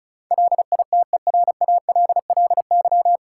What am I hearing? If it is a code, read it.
Morse code, heard as a keyed tone, about 36 wpm, 700 Hz. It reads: LITERALLY
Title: Sentences from Top 1400 Words - New Word Drill - 35wpm